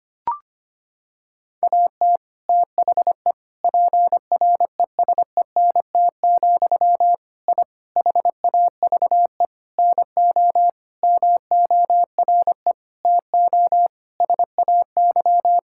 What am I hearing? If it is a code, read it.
Morse code, heard as a keyed tone, about 25 wpm, 700 Hz. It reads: AT THE PRESENT, I HAVE NO MORE TO SAY
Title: Sentences from Top 1000 Words - Review Part 5 - 25wpm